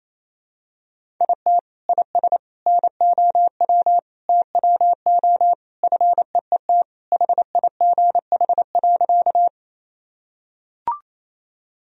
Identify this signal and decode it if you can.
Morse code, heard as a keyed tone, about 28 wpm, 700 Hz. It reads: IT IS NOW TWO FEET HIGH.
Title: Sentences from Top 1000 Words - Review Part 1 - 28wpm